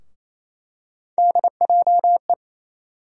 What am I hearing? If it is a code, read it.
Morse code, heard as a keyed tone, about 27 wpm, 700 Hz. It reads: DJE
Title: USA Call Sign Postfixes - 3 Letters 28wpm